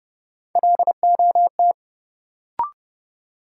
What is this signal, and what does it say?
Morse code, heard as a keyed tone, about 32 wpm, 700 Hz. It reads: LOT
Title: Sentences from Top 500 Words - New Word Drill - 30wpm